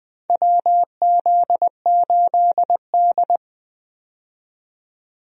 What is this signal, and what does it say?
Morse code, heard as a keyed tone, about 20 wpm, 700 Hz. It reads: WZ8D
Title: USA Call Signs - 4 Characters - 20wpm